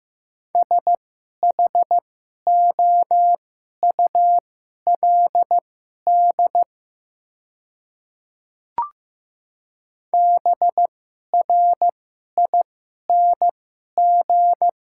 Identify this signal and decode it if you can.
Morse code, heard as a keyed tone, about 15 wpm, spaced as for 11 wpm, 700 Hz. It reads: SHOULD BRING
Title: 2x Character Spacing - Top 500 Words - 15wpm